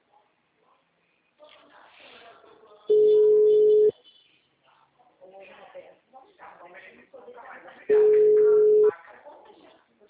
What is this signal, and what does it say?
Ringback tone (Europe, 425 Hz), with speech.